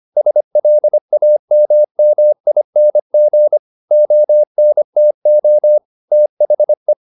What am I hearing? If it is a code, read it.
Morse code, heard as a keyed tone, about 25 wpm, 600 Hz. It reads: SLAMMING ONTO THE